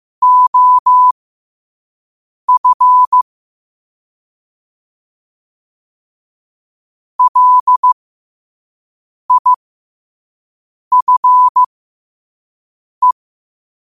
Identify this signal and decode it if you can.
Morse code, heard as a keyed tone, about 15 wpm, spaced as for 5 wpm, 1000 Hz. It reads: OF LIFE
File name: qotdfiles / 5wpm